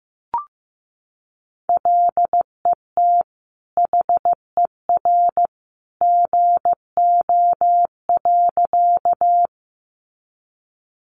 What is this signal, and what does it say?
Morse code, heard as a keyed tone, about 15 wpm, 700 Hz. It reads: LET HER GO.
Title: Sentences from Top 300 Words 15wpm